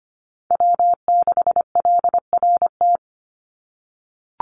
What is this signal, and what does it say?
Morse code, heard as a keyed tone, about 25 wpm, 700 Hz. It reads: W6LRT
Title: USA Call Signs 25wpm